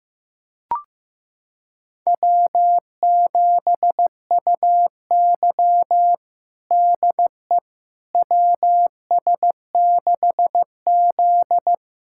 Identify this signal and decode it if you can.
Morse code, heard as a keyed tone, about 16 wpm, 700 Hz. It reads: W7UY DE WS6Z
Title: USA Callsign DE Callsign - 4 Characters - 15wpm